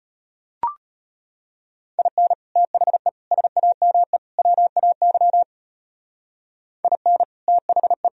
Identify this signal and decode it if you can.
Morse code, heard as a keyed tone, about 32 wpm, 700 Hz. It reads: IN THE SAME WAY IN THE